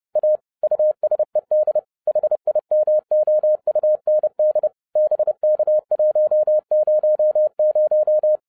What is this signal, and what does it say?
Morse code, heard as a keyed tone, about 30 wpm, 610 Hz. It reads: A USED HIMOUND BK100